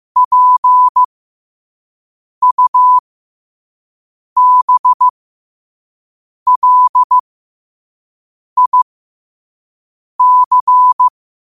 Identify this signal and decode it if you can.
Morse code, heard as a keyed tone, about 15 wpm, spaced as for 5 wpm, 1000 Hz. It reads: PUBLIC
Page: New quotes every day in morse code at 5 Words per minute.